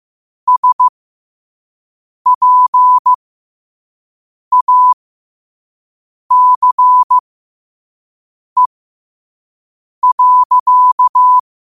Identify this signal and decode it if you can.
Morse code, heard as a keyed tone, about 15 wpm, spaced as for 5 wpm, 1000 Hz. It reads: SPACE.